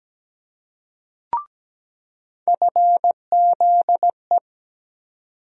Morse code, heard as a keyed tone, about 17 wpm, 700 Hz. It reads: FZE
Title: USA Call Sign Postfixes - 3 Letters 17wpm